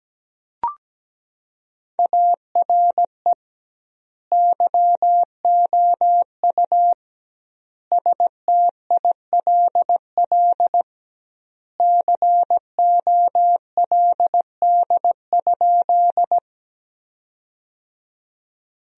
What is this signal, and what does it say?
Morse code, heard as a keyed tone, about 17 wpm, 700 Hz. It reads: ARE YOU STILL COLD?
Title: 2x Word-Spacing - Sentences from Top 400 Words - 17wpm